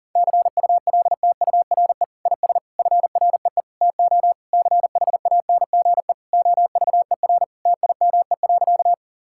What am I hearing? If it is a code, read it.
Morse code, heard as a keyed tone, about 40 wpm, 700 Hz. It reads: CULTURE IS FREE TO CHANGE OVER TIME.
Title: Sentences from Top 1000 Words - Review Part 6 - 40wpm